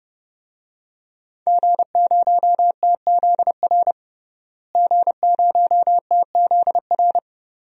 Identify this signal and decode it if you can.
Morse code, heard as a keyed tone, about 31 wpm, 700 Hz. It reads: G0TZR G0TZR
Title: DXCC-100 Call Signs - 5 Characters - Repeated 2x - 30wpm